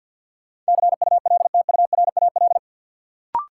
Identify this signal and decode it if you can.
Morse code, heard as a keyed tone, about 53 wpm, 700 Hz. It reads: CULTURAL
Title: Sentences from Top 1500 Words - New Word Drill - 50wpm